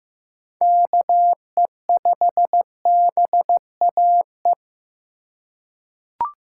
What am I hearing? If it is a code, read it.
Morse code, heard as a keyed tone, about 15 wpm, 700 Hz. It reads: KE5BAE
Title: USA Call Signs - 6 Characters - 4x Character Spacing - 15wpm